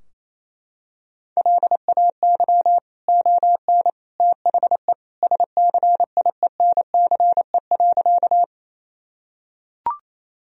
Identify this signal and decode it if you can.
Morse code, heard as a keyed tone, about 28 wpm, 700 Hz. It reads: LAY ON THE SCIENCE.
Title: Sentences from Top 1000 Words - Review Part 2 - 28wpm